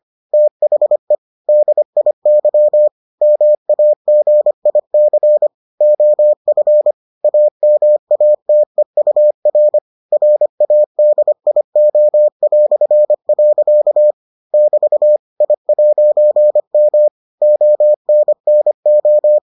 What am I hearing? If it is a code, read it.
Morse code, heard as a keyed tone, about 25 wpm, 600 Hz. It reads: THE DIY MAGIC OF AMATEUR RADIO". <BT> I'M ONNO